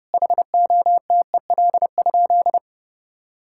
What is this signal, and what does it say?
Morse code, heard as a keyed tone, about 30 wpm, 700 Hz. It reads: HOTEL?
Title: Sentences from Top 1600 Words 30wpm